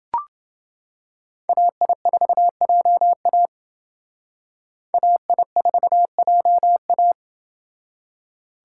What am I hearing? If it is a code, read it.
Morse code, heard as a keyed tone, about 28 wpm, 700 Hz. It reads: AI4JA AI4JA